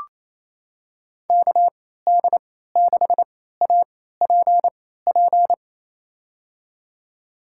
Morse code, heard as a keyed tone, about 28 wpm, spaced as for 16 wpm, 700 Hz. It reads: KD6APP
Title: USA Call Signs - 6 Characters - 3x Character Spacing - 28wpm